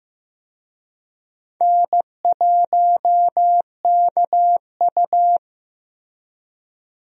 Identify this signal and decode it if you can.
Morse code, heard as a keyed tone, about 15 wpm, 700 Hz. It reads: N1KU
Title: USA Call Signs - 4 Characters - Repeated 2x - 15wpm